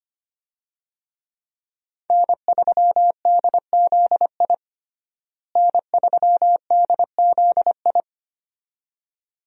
Morse code, heard as a keyed tone, about 25 wpm, 700 Hz. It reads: N3DZI N3DZI
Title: USA Call Signs - 5 Characters - Repeated 2x - 25wpm